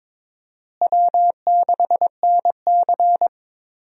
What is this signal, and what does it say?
Morse code, heard as a keyed tone, about 22 wpm, 700 Hz. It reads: W6NC
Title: USA Call Signs - 4 Characters - Repeated 3x - 22wpm